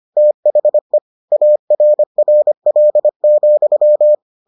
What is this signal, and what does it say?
Morse code, heard as a keyed tone, about 25 wpm, 600 Hz. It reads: THE ARRL,